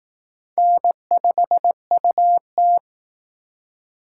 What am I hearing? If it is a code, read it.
Morse code, heard as a keyed tone, about 18 wpm, 700 Hz. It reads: N5UT
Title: Speed Racing - USA 1x2 Call Signs 15wpm